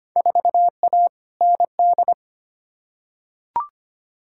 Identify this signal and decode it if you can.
Morse code, heard as a keyed tone, about 25 wpm, 700 Hz. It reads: 4A ND
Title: Field Day Exchanges Encore - 25wpm